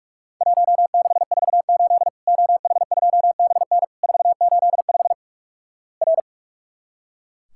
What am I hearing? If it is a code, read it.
Morse code, heard as a keyed tone, about 43 wpm, 700 Hz. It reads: 1648 OH2BN 485 R
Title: Worked All Europe DX Contest - QTC Exchange - 45wpm